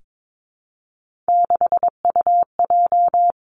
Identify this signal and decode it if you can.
Morse code, heard as a keyed tone, about 22 wpm, 700 Hz. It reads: 6UJ
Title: Three Challenging Letter-Number Combinations 22wpm